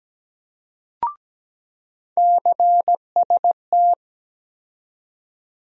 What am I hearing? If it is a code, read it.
Morse code, heard as a keyed tone, about 17 wpm, 700 Hz. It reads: CST
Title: USA Call Sign Postfixes - 3 Letters 17wpm